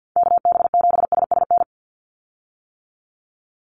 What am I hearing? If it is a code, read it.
Morse code, heard as a keyed tone, about 50 wpm, 700 Hz. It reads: KB7SSN
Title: USA Call Signs 50wpm